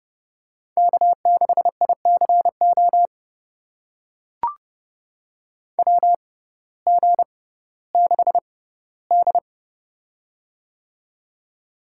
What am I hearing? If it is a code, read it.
Morse code, heard as a keyed tone, about 30 wpm, spaced as for 16 wpm, 700 Hz. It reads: K6ICO WG6D
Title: USA Call Signs - 6x Character Spacing - 30wpm